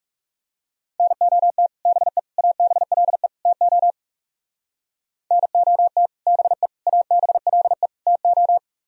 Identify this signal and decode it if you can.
Morse code, heard as a keyed tone, about 39 wpm, 700 Hz. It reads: NOT BE ABLE TO NOT BE ABLE TO